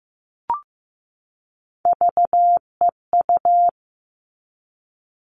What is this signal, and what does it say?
Morse code, heard as a keyed tone, about 15 wpm, 700 Hz. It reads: VEU
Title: All Cut Numbers - 3-Digit 15wpm